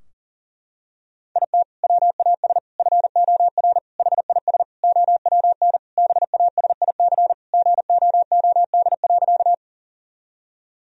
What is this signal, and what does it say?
Morse code, heard as a keyed tone, about 40 wpm, 700 Hz. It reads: IT WAS FOR HIS OWN BASIC GOOD.
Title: Sentences from Top 1000 Words - Review Part 3 - 40wpm